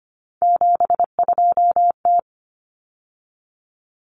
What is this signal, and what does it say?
Morse code, heard as a keyed tone, about 25 wpm, 700 Hz. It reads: 72T
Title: Common Cut Numbers - 3-Digit 25wpm